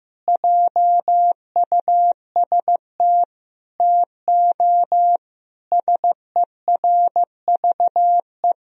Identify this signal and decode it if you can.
Morse code, heard as a keyed tone, about 15 wpm, 700 Hz. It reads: JUST TO SERVE
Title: Sentences from Top 1000 Words - Review Part 2 - 15wpm